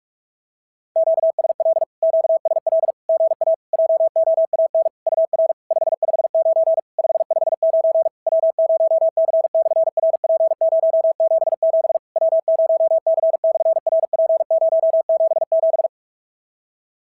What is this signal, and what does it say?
Morse code, heard as a keyed tone, about 51 wpm, 650 Hz. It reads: QSL QSL GA JOAN UR 559 559 W0C/RP077 W0C/RP077
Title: SOTA QSO Practice 45wpm